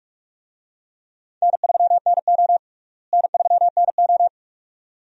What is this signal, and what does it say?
Morse code, heard as a keyed tone, about 47 wpm, 700 Hz. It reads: N3NO N3NO
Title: USA Call Signs - 4 Characters - Repeated 2x - 45wpm